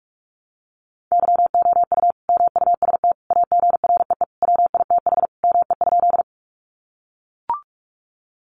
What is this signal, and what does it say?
Morse code, heard as a keyed tone, about 44 wpm, 700 Hz. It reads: YOU MUST AGREE WITH ME?
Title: Sentences from Top 1100 Words 45wpm